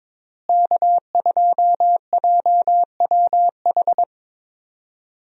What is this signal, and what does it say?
Morse code, heard as a keyed tone, about 21 wpm, 700 Hz. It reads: K2JWH
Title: USA Call Signs - 5 Characters - Repeated 2x - 22wpm